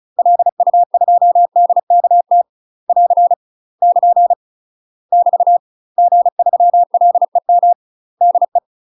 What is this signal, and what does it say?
Morse code, heard as a keyed tone, about 34 wpm, 700 Hz. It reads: LU2DKT <AR> <KN> <BT> G3LEM DE
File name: UK MASTER QSOs TRAINING/35WPM